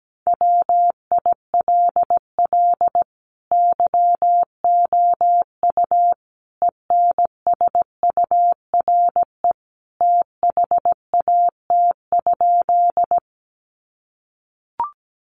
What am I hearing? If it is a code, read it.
Morse code, heard as a keyed tone, about 17 wpm, 700 Hz. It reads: WILL YOU ENSURE THAT?
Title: Sentences from Top 1100 Words 17wpm